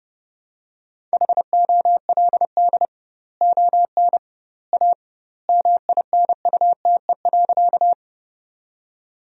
Morse code, heard as a keyed tone, about 30 wpm, 700 Hz. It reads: HOLD ON A MINUTE.
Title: 2x Word-Spacing - Sentences from Top 500 Words - 30wpm